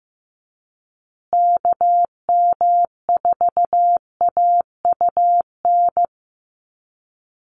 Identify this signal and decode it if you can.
Morse code, heard as a keyed tone, about 15 wpm, 700 Hz. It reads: KM4AUN
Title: USA Call Signs Repeated 2x 15wpm